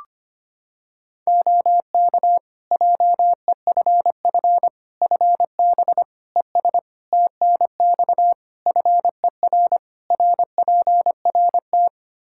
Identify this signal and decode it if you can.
Morse code, heard as a keyed tone, about 25 wpm, 700 Hz. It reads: OK JEFF FB ES TNX FER RPRT
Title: K1USN SST QSO Practice 25wpm